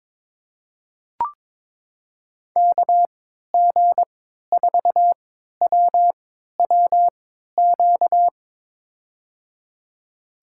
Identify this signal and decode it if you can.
Morse code, heard as a keyed tone, about 22 wpm, spaced as for 13 wpm, 700 Hz. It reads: KG4WWQ
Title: USA Call Signs - 6 Characters - 3x Character Spacing - 22wpm